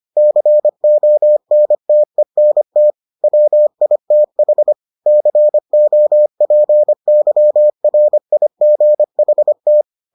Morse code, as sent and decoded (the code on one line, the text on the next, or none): -.-. --- -. - . -. - / .-- .. - .... / -.-. --- .--. -.-- .-. .. --. .... -
CONTENT WITH COPYRIGHT